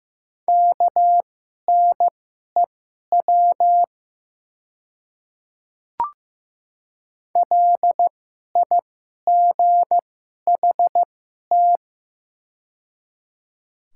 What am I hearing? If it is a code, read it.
Morse code, heard as a keyed tone, about 15 wpm, spaced as for 11 wpm, 700 Hz. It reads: KNEW LIGHT
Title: 2x Character Spacing - Top 500 Words - 15wpm